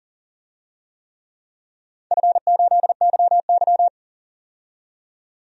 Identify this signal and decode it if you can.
Morse code, heard as a keyed tone, about 39 wpm, 700 Hz. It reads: F8YY
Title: DXCC-100 Call Signs - 4 Characters - 40wpm